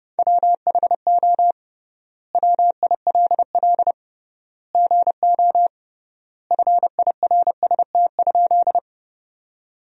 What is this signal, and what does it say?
Morse code, heard as a keyed tone, about 30 wpm, 700 Hz. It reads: WHO WILL GO FIRST?
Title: 3x Word-Spacing - Sentences from Top 100 Words - 30wpm